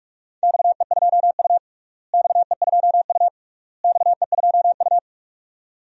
Morse code, heard as a keyed tone, about 46 wpm, 700 Hz. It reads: XE2U XE2U XE2U
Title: DXCC-100 Call Signs - 4 Characters - Repeated 3x - 45wpm